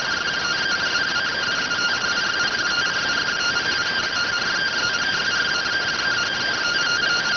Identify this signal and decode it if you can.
Two-tone teleprinter signal, about 170 Hz shift.